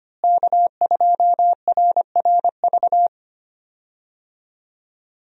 Morse code, heard as a keyed tone, about 25 wpm, 700 Hz. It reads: K2RRV
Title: USA Call Signs - 5 Characters - Repeated 3x - 25wpm